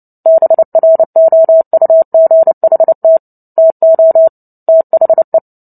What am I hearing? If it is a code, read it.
Morse code, heard as a keyed tone, about 30 wpm, 650 Hz. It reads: BROUGHT TO THE
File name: Book Two / 03-Chapter 3 - 29 WPM